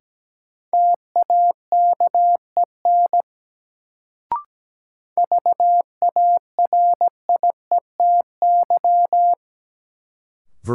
Morse code, heard as a keyed tone, about 17 wpm, 700 Hz. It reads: TAKEN VARIETY
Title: Sentences from Top 1000 Words - Review Part 4 - New Word Drill - 17wpm